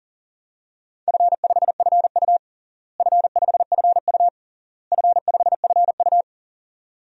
Morse code, heard as a keyed tone, about 40 wpm, 700 Hz. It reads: F5FU F5FU F5FU
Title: DXCC-100 Call Signs - 4 Characters - Repeated 3x - 40wpm